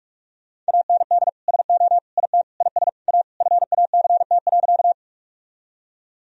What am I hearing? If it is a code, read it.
Morse code, heard as a keyed tone, about 46 wpm, 700 Hz. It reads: AND SO IT IS A FACT.
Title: Sentences from Top 1000 Words - Review Part 3 - 45wpm